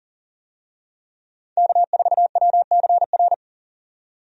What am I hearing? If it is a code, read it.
Morse code, heard as a keyed tone, about 40 wpm, 700 Hz. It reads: K4WCR
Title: USA Call Signs - 5 Characters - 40wpm